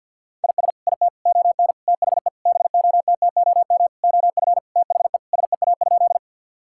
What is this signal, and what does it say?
Morse code, heard as a keyed tone, about 50 wpm, 700 Hz. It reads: IS IT ON THE BOTTOM OF THE SEA?
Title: Sentences from Top 1000 Words - Review Part 6 - 50wpm